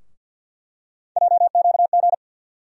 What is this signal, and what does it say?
Morse code, heard as a keyed tone, about 50 wpm, 700 Hz. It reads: JQG
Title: USA Call Sign Postfixes - 3 Letters 50wpm